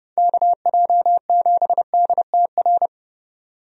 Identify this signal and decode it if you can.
Morse code, heard as a keyed tone, about 30 wpm, 700 Hz. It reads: KJ7DTR
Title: USA Call Signs - 6 Characters - Repeated 3x - 30wpm